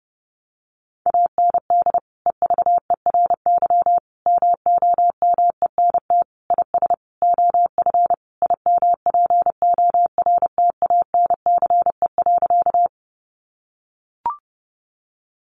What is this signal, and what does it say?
Morse code, heard as a keyed tone, about 30 wpm, 700 Hz. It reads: AND EVERY MOMENT IS OF IMPORTANCE.
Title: Sentences from Top 1200 Words 30wpm